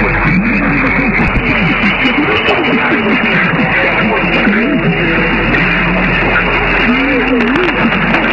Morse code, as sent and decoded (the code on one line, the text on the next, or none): none